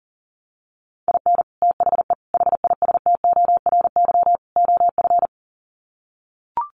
Morse code, heard as a keyed tone, about 40 wpm, 700 Hz. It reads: IN THE HISTORY OF
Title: Sets of 4 Words 40wpm